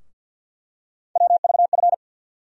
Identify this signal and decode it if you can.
Morse code, heard as a keyed tone, about 52 wpm, 700 Hz. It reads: WVF
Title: USA Call Sign Postfixes - 3 Letters 50wpm